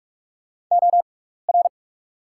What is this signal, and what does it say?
Morse code, heard as a keyed tone, about 44 wpm, spaced as for 15 wpm, 690 Hz. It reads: OR
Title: 6x Character Spacing - Top 200 Words - 45wpm